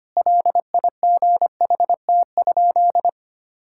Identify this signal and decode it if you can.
Morse code, heard as a keyed tone, about 25 wpm, 700 Hz. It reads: LIGHT?
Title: Sentences from Top 1000 Words - Review Part 5 - 25wpm